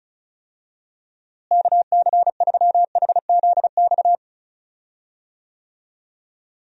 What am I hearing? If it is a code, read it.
Morse code, heard as a keyed tone, about 36 wpm, 700 Hz. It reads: KC3HZX
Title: USA Call Signs - 6 Characters - Repeated 3x - 35wpm